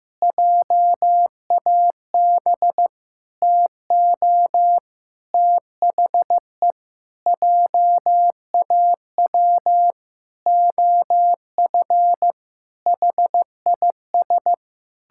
Morse code, heard as a keyed tone, about 15 wpm, 700 Hz. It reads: JAB TO THE JAW OF HIS
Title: Pangram Challenge v2 - 15wpm